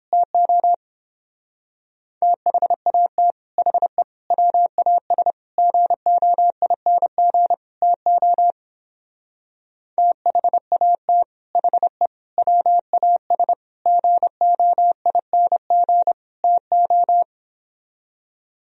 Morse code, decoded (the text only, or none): TO THAT HE WAS GOING TO THAT HE WAS GOING TO